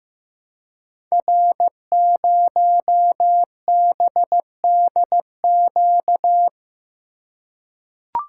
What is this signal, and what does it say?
Morse code, heard as a keyed tone, about 15 wpm, 700 Hz. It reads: R0BDQ
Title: Code Groups - 5 Letter:Numbers - 15wpm